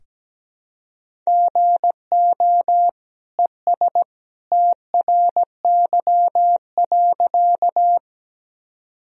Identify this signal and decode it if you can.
Morse code, heard as a keyed tone, about 17 wpm, 700 Hz. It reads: GO ES TRY.
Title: Sentences from Top 200 Words and Abbreviations 17wpm